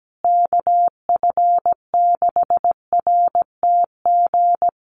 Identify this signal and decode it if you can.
Morse code, heard as a keyed tone, about 17 wpm, 700 Hz. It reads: KF6RTG
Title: USA Call Signs Repeated 3x 17wpm